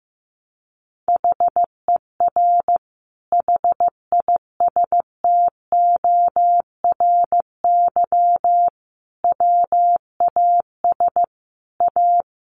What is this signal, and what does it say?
Morse code, heard as a keyed tone, about 15 wpm, 700 Hz. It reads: HER HISTORY WAS A
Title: Sentences from Top 1300 Words 15wpm